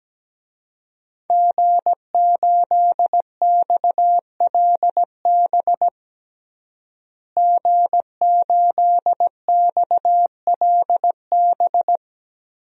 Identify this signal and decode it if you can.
Morse code, heard as a keyed tone, about 17 wpm, 700 Hz. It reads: G8XLB G8XLB
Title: DXCC-100 Call Signs - 5 Characters - Repeated 3x - 17wpm